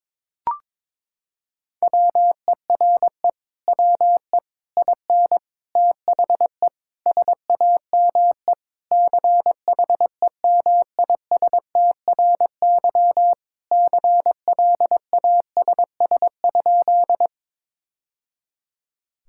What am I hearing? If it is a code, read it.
Morse code, heard as a keyed tone, about 22 wpm, 700 Hz. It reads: WERE WE IN THE SAME CHEMISTRY CLASS?
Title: Sentences from Top 1000 Words - Review Part 6 - 22wpm